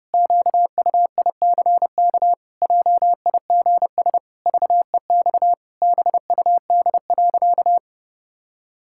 Morse code, heard as a keyed tone, about 30 wpm, 700 Hz. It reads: QUICK JIGS VEX BUD.
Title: Pangram Challenge v2 - 30wpm